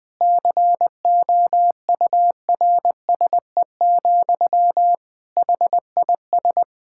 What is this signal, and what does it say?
Morse code, heard as a keyed tone, about 20 wpm, 700 Hz. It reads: COURSE, HIS
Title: Sentences from Top 1000 Words - Review Part 5 - 20wpm